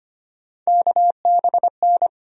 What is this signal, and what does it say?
Morse code, heard as a keyed tone, about 25 wpm, 700 Hz. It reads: KBN
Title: USA Call Sign Postfixes - 3 Letters 25wpm